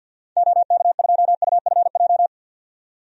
Morse code, heard as a keyed tone, about 51 wpm, 700 Hz. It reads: OK2UFJ
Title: DXCC-100 Call Signs - 6 Characters - 50wpm